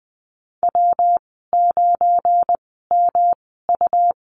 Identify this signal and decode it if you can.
Morse code, heard as a keyed tone, about 20 wpm, spaced as for 15 wpm, 700 Hz. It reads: W9MU
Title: 179 Lesson 29g - Slash - Call Signs 20wpm